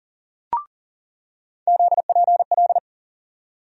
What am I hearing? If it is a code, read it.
Morse code, heard as a keyed tone, about 40 wpm, 700 Hz. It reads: ZPL
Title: USA Call Sign Postfixes - 3 Letters 40wpm